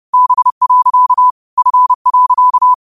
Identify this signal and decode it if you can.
Morse code, heard as a keyed tone, about 30 wpm, 1000 Hz. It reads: B. F.
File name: qotdfiles / 30wpm